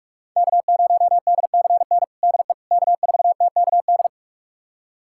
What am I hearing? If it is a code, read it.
Morse code, heard as a keyed tone, about 44 wpm, 700 Hz. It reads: K0DCN DE K4TKD
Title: USA Callsign DE Callsign - 5 Characters - 45wpm